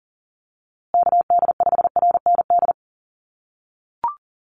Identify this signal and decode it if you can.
Morse code, heard as a keyed tone, about 40 wpm, 700 Hz. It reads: KD5RND
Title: USA Call Signs Repeated 2x 40wpm